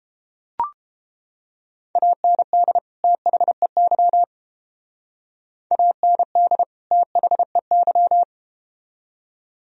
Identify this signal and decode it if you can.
Morse code, heard as a keyed tone, about 31 wpm, 700 Hz. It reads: AND THEY AND THEY